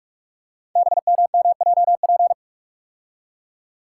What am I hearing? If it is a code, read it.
Morse code, heard as a keyed tone, about 44 wpm, 690 Hz. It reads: BMMJP
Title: Code Groups - 5 Letter:Numbers - 45wpm